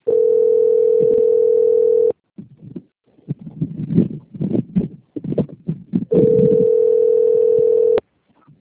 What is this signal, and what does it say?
Ringback tone (North America), 440 + 480 Hz.
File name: out- - - - - .31170